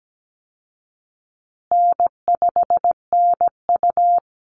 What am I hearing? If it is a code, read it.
Morse code, heard as a keyed tone, about 17 wpm, 700 Hz. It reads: N5NU
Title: Easy Call Signs Repeated 3x - 17wpm